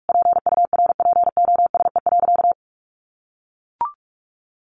Morse code, heard as a keyed tone, about 44 wpm, 700 Hz. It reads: PURPOSE.
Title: Sentences from Top 800 Words 45wpm